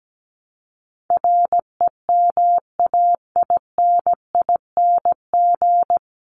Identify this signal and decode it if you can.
Morse code, heard as a keyed tone, about 17 wpm, 700 Hz. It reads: REMAINING
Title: ING Suffix - Top 100 Words with Base Words 17wpm